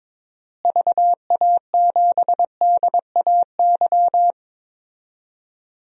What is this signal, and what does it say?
Morse code, heard as a keyed tone, about 21 wpm, 700 Hz. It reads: VA7DAY
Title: DXCC-100 Call Signs - 6 Characters - Repeated 3x - 22wpm